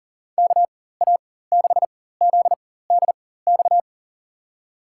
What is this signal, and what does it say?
Morse code, heard as a keyed tone, about 40 wpm, spaced as for 19 wpm, 700 Hz. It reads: KA6ZDX
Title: USA Call Signs - 6 Characters - 4x Character Spacing - 40wpm